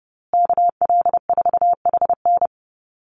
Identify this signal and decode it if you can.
Morse code, heard as a keyed tone, about 30 wpm, 700 Hz. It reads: KL4HN
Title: USA Call Signs Repeated 3x 30wpm